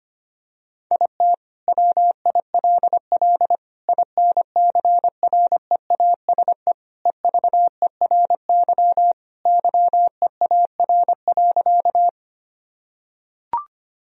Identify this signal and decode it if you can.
Morse code, heard as a keyed tone, about 25 wpm, 700 Hz. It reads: IT WILL INCREASE EVERY YEAR.
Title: Sentences from Top 1000 Words - Review Part 4 - 25wpm